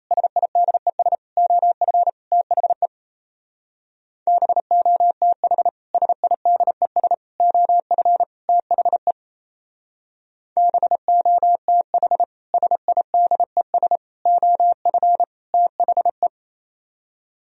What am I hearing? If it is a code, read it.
Morse code, heard as a keyed tone, about 35 wpm, 700 Hz. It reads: SIDES OF THE BOTH SIDES OF THE BOTH SIDES OF THE
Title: Speed-Racing - Sets of 4 Words 28wpm